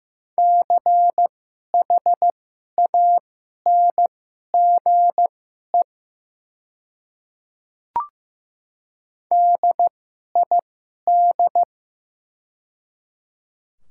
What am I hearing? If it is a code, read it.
Morse code, heard as a keyed tone, about 15 wpm, spaced as for 11 wpm, 700 Hz. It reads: CHANGE DID
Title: 2x Character Spacing - Top 300 Words - 15wpm